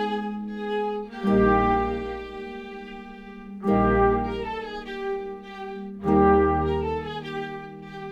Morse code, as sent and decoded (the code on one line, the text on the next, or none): none